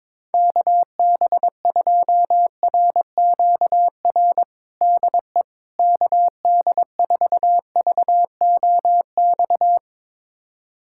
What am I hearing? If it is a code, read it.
Morse code, heard as a keyed tone, about 22 wpm, 700 Hz. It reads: KB2RQR DE KD4VOX
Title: USA Callsign DE Callsign - 6 Characters - 22wpm